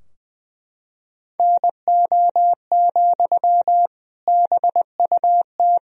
Morse code, heard as a keyed tone, about 20 wpm, 700 Hz. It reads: NO, BUT
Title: Sentences from Top 1000 Words - Review Part 2 - 20wpm